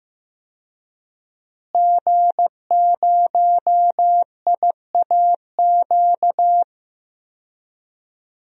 Morse code, heard as a keyed tone, about 15 wpm, 700 Hz. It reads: G0IAQ
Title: DXCC-100 Call Signs - 5 Characters - Repeated 3x - 15wpm